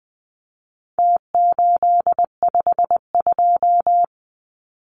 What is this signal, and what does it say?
Morse code, heard as a keyed tone, about 20 wpm, 700 Hz. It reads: T852
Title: Numbers - Contest Style - 4-Digit 20wpm